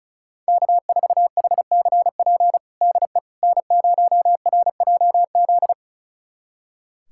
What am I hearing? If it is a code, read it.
Morse code, heard as a keyed tone, about 36 wpm, 700 Hz. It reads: K4HCP DE N0RJZ
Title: USA Callsign DE Callsign - 5 Characters - 35wpm